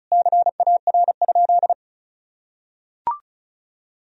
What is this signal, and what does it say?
Morse code, heard as a keyed tone, about 36 wpm, 700 Hz. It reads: CAR?
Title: Sentences from Top 1000 Words - Review Part 1 - 35wpm